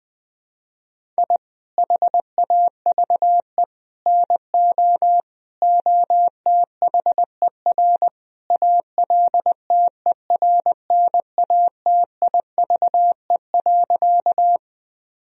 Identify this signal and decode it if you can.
Morse code, heard as a keyed tone, about 20 wpm, 700 Hz. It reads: I HAVE NO OTHER ALTERNATIVE.
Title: Sentences from Top 1000 Words - Review Part 4 - 20wpm